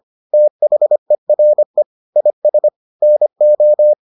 Morse code, heard as a keyed tone, about 25 wpm, 600 Hz. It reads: THERE IS NO